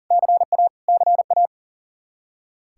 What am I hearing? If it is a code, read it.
Morse code, heard as a keyed tone, about 40 wpm, 700 Hz. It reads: CA CA
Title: Practice QSO - Exchange 1 - Beginning - 40wpm